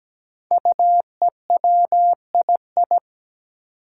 Morse code, heard as a keyed tone, about 17 wpm, 700 Hz. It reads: UEWII
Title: Five Challenging Letter-Number Combinations 17wpm